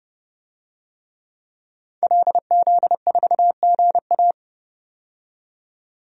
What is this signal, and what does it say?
Morse code, heard as a keyed tone, about 30 wpm, 700 Hz. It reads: LZ4GA
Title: DXCC-100 Call Signs - 5 Characters - 30wpm